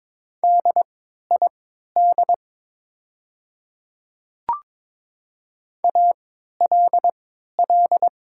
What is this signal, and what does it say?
Morse code, heard as a keyed tone, about 22 wpm, spaced as for 12 wpm, 700 Hz. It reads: DID ALL
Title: 3x Character Spacing - Top 100 Words - 22wpm